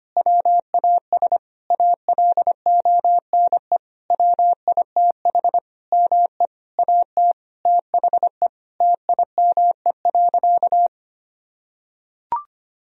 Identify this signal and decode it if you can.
Morse code, heard as a keyed tone, about 25 wpm, 690 Hz. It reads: WAS ALONE WITH ME AT THE TIME.
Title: Sentences from Top 1000 Words - Review Part 4 - 25wpm